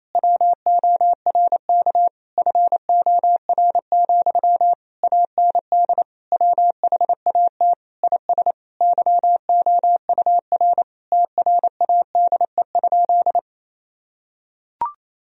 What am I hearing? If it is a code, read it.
Morse code, heard as a keyed tone, about 28 wpm, 700 Hz. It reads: WORK FOR, AND WHAT IS YOUR TRADE?
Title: Sentences from Top 1000 Words - Review Part 4 - 28wpm